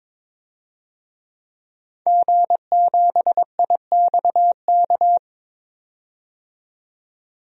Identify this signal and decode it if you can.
Morse code, heard as a keyed tone, about 21 wpm, 700 Hz. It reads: G7IXK
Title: DXCC-100 Call Signs - 5 Characters - 22wpm